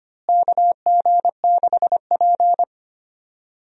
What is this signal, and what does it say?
Morse code, heard as a keyed tone, about 25 wpm, 700 Hz. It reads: KG6P
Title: USA Call Signs - 4 Characters - Repeated 2x - 25wpm